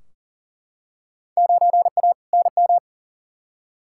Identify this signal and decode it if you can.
Morse code, heard as a keyed tone, about 40 wpm, 700 Hz. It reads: 9A NM
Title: Field Day Exchanges Encore - 40wpm